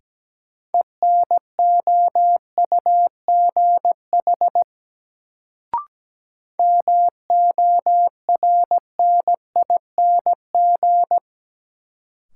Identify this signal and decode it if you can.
Morse code, heard as a keyed tone, about 17 wpm, 700 Hz. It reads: ENOUGH MORNING
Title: Sentences from Top 400 Words - New Word Drill - 17wpm